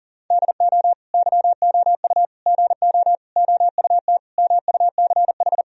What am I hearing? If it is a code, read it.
Morse code, heard as a keyed tone, about 40 wpm, 700 Hz. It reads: DO YOU GO OUT MUCH
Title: Sentences from Top 1000 Words - Review Part 4 - 40wpm